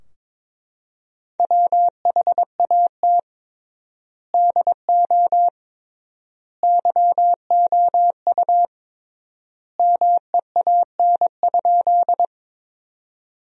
Morse code, heard as a keyed tone, about 22 wpm, 700 Hz. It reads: WHAT DO YOU MEAN?
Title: 3x Word-Spacing - Sentences from Top 200 Words - 22wpm